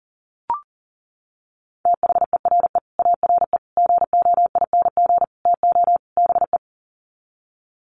Morse code, heard as a keyed tone, about 40 wpm, 700 Hz. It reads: THERE ARE GOING TO BE
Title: Sets of 5 Words 40wpm